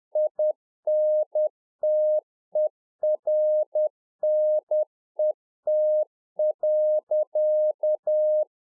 Morse code, heard as a keyed tone, about 10 wpm, 610 Hz. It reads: INTERNET.